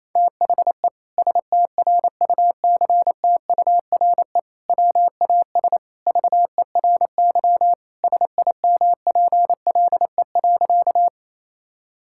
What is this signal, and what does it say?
Morse code, heard as a keyed tone, about 28 wpm, 700 Hz. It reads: THE STRUCTURE WAS VERY SIMPLE.
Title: Sentences from Top 1000 Words - Review Part 6 - 28wpm